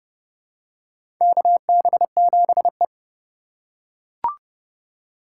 Morse code, heard as a keyed tone, about 30 wpm, 700 Hz. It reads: KB7E
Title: USA Call Signs - 4 Characters - 6x Character Spacing - 30wpm